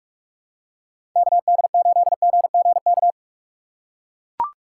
Morse code, heard as a keyed tone, about 41 wpm, 700 Hz. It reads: KD8GGK
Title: USA Call Signs - 6 Characters - 2x Character Spacing - 45wpm